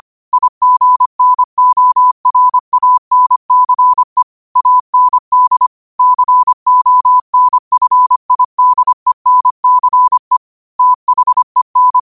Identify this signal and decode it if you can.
Morse code, heard as a keyed tone, about 25 wpm, 1000 Hz. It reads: IGNORANCE AND CONFIDENCE THEN
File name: qotdfiles / 25wpm